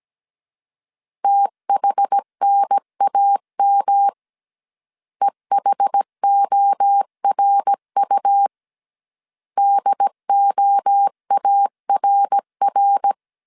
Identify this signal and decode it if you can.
Morse code, heard as a keyed tone, about 17 wpm, 800 Hz. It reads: THNAM EHORU DOARR
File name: CW / DET / 1 3 1500C S C 16wpm 2148